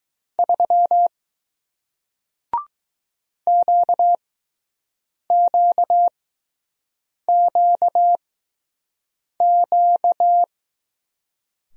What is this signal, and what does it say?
Morse code, heard as a keyed tone, about 20 wpm, spaced as for 6 wpm, 700 Hz. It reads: 3 QQQQ